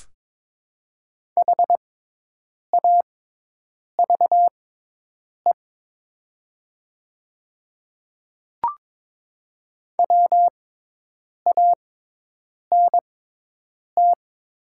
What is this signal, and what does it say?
Morse code, heard as a keyed tone, about 22 wpm, spaced as for 8 wpm, 700 Hz. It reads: HAVE WANT
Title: 6x Character Spacing - Top 300 Words - 22wpm